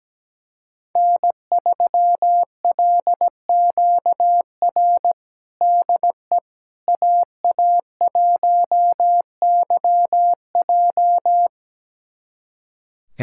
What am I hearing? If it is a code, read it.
Morse code, heard as a keyed tone, about 17 wpm, 700 Hz. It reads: N3LQR DE AA1YJ
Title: USA Callsign DE Callsign - 5 Characters - 17wpm